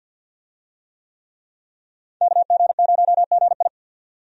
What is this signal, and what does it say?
Morse code, heard as a keyed tone, about 51 wpm, 700 Hz. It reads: KG0GI
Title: USA Call Signs - 5 Characters - 50wpm